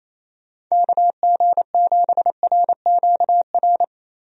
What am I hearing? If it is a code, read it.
Morse code, heard as a keyed tone, about 27 wpm, 700 Hz. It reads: KG7RQR
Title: USA Call Signs - 6 Characters - 28wpm